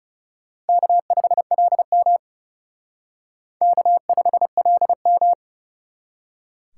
Morse code, heard as a keyed tone, about 36 wpm, 700 Hz. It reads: K5LM K5LM
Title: Speed Racing - USA 1x2 Call Signs 30wpm